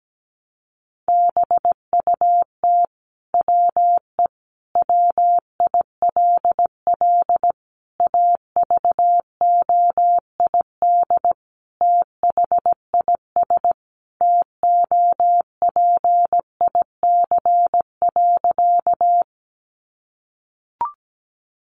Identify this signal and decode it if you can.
Morse code, heard as a keyed tone, about 17 wpm, 700 Hz. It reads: BUT WE WILL AVOID THIS TOPIC.
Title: Sentences from Top 1200 Words 17wpm